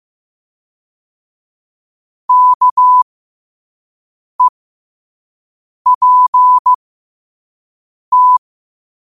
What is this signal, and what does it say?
Morse code, heard as a keyed tone, about 15 wpm, spaced as for 5 wpm, 1000 Hz. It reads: KEPT
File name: qotdfiles / 5wpm